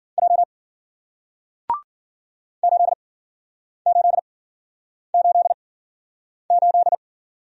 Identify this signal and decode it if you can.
Morse code, heard as a keyed tone, about 53 wpm, spaced as for 9 wpm, 700 Hz. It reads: J 8888